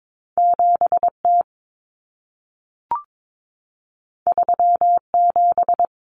Morse code, heard as a keyed tone, about 22 wpm, 700 Hz. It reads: 7T 37
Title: Common Cut Numbers - 2-Digit 22wpm